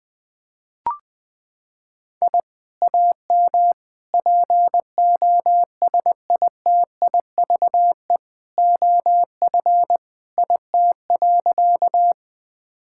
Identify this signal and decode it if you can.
Morse code, heard as a keyed tone, about 20 wpm, 700 Hz. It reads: I AM POSITIVE OF IT.
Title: Sentences from Top 1000 Words - Review Part 5 - 20wpm